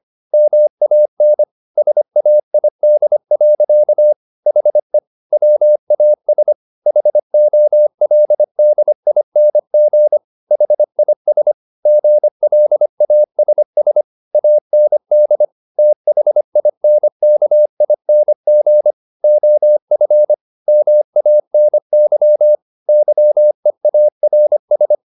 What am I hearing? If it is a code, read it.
Morse code, heard as a keyed tone, about 25 wpm, 600 Hz. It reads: MAN SAID. HE WAS HOLDING HIS GLASS AND THINKING OF MANY YEARS